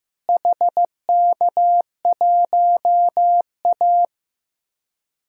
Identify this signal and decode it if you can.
Morse code, heard as a keyed tone, about 15 wpm, 700 Hz. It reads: HK1A
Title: DXCC-100 Call Signs - 4 Characters - Repeated 2x - 15wpm